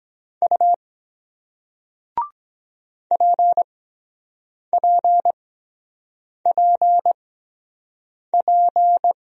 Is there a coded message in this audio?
Yes: Morse code from 0.4 to 9.1 s, heard as a keyed tone, about 23 wpm, spaced as for 7 wpm, 700 Hz. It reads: U PPPP